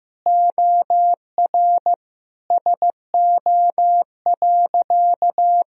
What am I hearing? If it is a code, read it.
Morse code, heard as a keyed tone, about 14 wpm, 700 Hz. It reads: OR SO.
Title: Sentences from Top 1000 Words - Review Part 1 - 15wpm